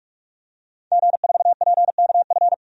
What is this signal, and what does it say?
Morse code, heard as a keyed tone, about 46 wpm, 700 Hz. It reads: G4PKF
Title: DXCC-100 Call Signs - 5 Characters - 45wpm